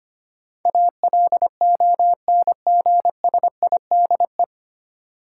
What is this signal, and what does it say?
Morse code, heard as a keyed tone, about 25 wpm, 700 Hz. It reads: ALONGSIDE
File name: practice / 25wpm / 9 Letter Words - First 3 Letters Given - 25wpm